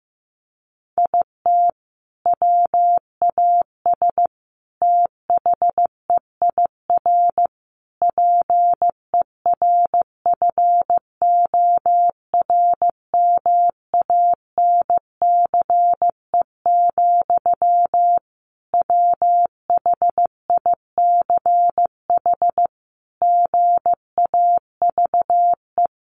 Morse code, heard as a keyed tone, about 15 wpm, 700 Hz. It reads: IT WAS THEIR PERFORMANCE, WHICH GAVE
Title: Sentences from Top 1500 Words 15wpm